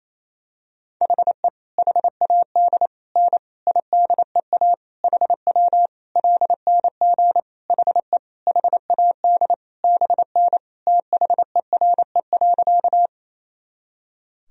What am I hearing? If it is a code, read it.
Morse code, heard as a keyed tone, about 28 wpm, 700 Hz. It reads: HE HAD N IDEA HW LNG HE HAD BN THERE.
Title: Sentences from Top 300 Words and Abbreviations 28wpm